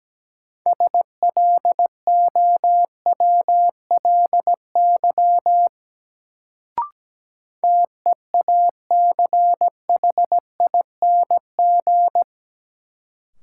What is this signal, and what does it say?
Morse code, heard as a keyed tone, about 17 wpm, 700 Hz. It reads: SLOWLY TEACHING
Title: Sentences from Top 1100 Words - New Word Drill - 17wpm